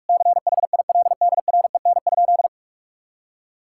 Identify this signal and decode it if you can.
Morse code, heard as a keyed tone, about 42 wpm, 700 Hz. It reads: CHILDREN?
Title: Sentences from Top 1000 Words - Review Part 5 - 45wpm